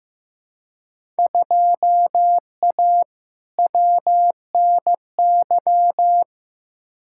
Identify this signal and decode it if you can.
Morse code, heard as a keyed tone, about 15 wpm, 700 Hz. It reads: 2A WNY
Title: Field Day Exchanges - 15wpm